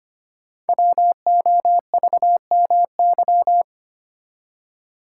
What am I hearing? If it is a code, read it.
Morse code, heard as a keyed tone, about 25 wpm, 700 Hz. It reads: WOVMY
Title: Code Groups - 5 Letters - 25wpm